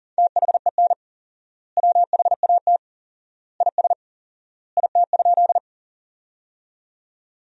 Morse code, heard as a keyed tone, about 37 wpm, 700 Hz. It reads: THEN WHAT IS IT?
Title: 4x Word-Spacing - Sentences from Top 100 Words - 40wpm